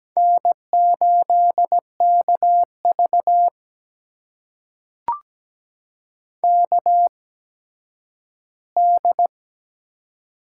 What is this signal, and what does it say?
Morse code, heard as a keyed tone, about 17 wpm, spaced as for 10 wpm, 700 Hz. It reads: N8KV KD
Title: USA Call Signs - 8x Character Spacing - 17wpm